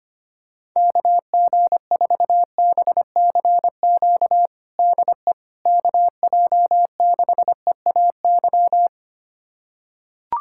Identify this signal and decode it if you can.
Morse code, heard as a keyed tone, about 25 wpm, 700 Hz. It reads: KG4BCQ DE KJ6EAY
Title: USA Callsign DE Callsign - 6 Characters - 25wpm